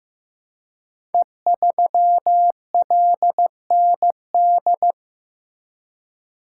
Morse code, heard as a keyed tone, about 15 wpm, 700 Hz. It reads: E3LND
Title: Code Groups - 5 Letter:Numbers - 15wpm